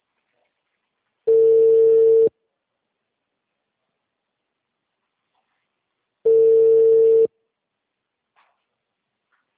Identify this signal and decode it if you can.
Ringback tone (China, 450 Hz).